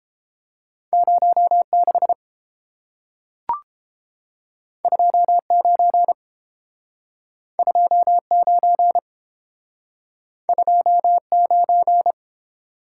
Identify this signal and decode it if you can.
Morse code, heard as a keyed tone, about 30 wpm, 700 Hz. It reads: 06 29 29 29